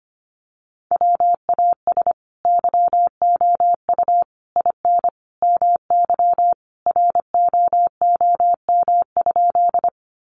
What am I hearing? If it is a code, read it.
Morse code, heard as a keyed tone, about 25 wpm, 700 Hz. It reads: WAS YOU IN MY ROOM?
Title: Sentences from Top 300 Words 25wpm